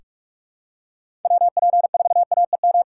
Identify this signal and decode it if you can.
Morse code, heard as a keyed tone, about 46 wpm, 700 Hz. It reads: WP4AEM
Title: USA Call Signs - 2x Character Spacing - 45wpm